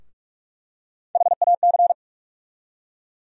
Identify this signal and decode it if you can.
Morse code, heard as a keyed tone, about 42 wpm, 700 Hz. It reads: HAC
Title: USA Call Sign Postfixes - 3 Letters 45wpm